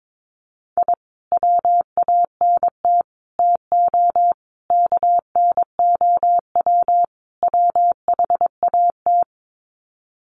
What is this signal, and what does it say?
Morse code, heard as a keyed tone, about 22 wpm, 700 Hz. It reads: I WANT TO KNOW WHAT
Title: Sets of 5 Words Encore2 22wpm